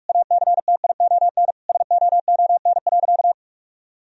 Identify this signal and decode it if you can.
Morse code, heard as a keyed tone, about 46 wpm, 700 Hz. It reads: ACTION SOON.
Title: Sentences from Top 1000 Words - Review Part 4 - 45wpm